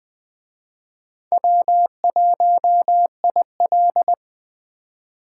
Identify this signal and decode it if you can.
Morse code, heard as a keyed tone, about 20 wpm, 700 Hz. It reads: W1IL
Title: Speed Racing - USA 1x2 Call Signs 17wpm